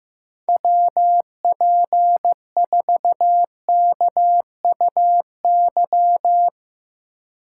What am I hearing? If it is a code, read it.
Morse code, heard as a keyed tone, about 16 wpm, 700 Hz. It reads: WP4KUY
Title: USA Call Signs - 6 Characters - 3x Character Spacing - 15wpm